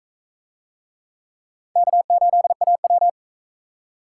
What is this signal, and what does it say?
Morse code, heard as a keyed tone, about 40 wpm, 700 Hz. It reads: K8AW